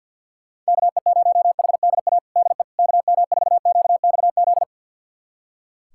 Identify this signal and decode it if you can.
Morse code, heard as a keyed tone, about 46 wpm, 700 Hz. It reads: KE0HDA DE KM4QXZ
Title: USA Callsign DE Callsign - 6 Characters - 50wpm